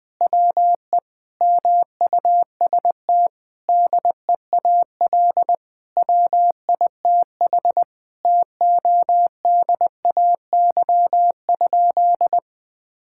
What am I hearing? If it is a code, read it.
Morse code, heard as a keyed tone, about 20 wpm, 700 Hz. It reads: WE MUST DEAL WITH TODAY?
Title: Sentences from Top 1000 Words - Review Part 6 - 20wpm